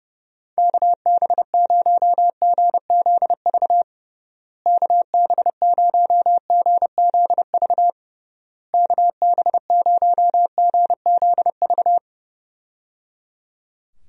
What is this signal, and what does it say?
Morse code, heard as a keyed tone, about 30 wpm, 700 Hz. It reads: KB0GZV KB0GZV KB0GZV
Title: USA Call Signs - 6 Characters - Repeated 3x - 30wpm